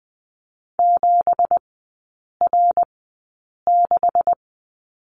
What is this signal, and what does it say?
Morse code, heard as a keyed tone, about 20 wpm, spaced as for 8 wpm, 690 Hz. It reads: 7R6
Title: 166 Lesson 27e - Number 7 - Three Characters 20wpm